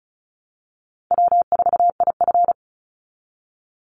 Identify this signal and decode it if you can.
Morse code, heard as a keyed tone, about 34 wpm, 700 Hz. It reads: W4IF
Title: Call Signs Repeated 2x - 35wpm